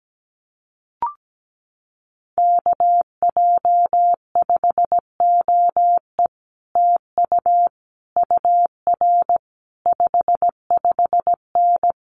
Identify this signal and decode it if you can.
Morse code, heard as a keyed tone, about 17 wpm, 700 Hz. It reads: KJ5OE TU UR 55N
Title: Common QSO Phrases Encore 17wpm